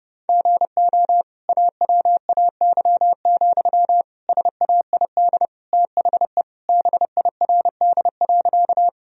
Morse code, heard as a keyed tone, about 29 wpm, 700 Hz. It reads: GO AWAY, SAID THE BIRD.
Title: Sentences from Top 1000 Words - Review Part 2 - 30wpm